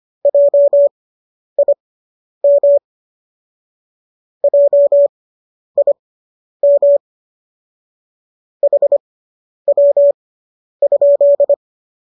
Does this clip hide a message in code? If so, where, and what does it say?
Yes: Morse code from 0.3 to 11.5 s, heard as a keyed tone, about 25 wpm, spaced as for 10 wpm, 570 Hz. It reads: JIM JIM HW?